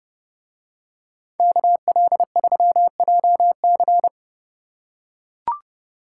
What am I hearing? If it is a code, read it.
Morse code, heard as a keyed tone, about 31 wpm, 700 Hz. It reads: KL3JC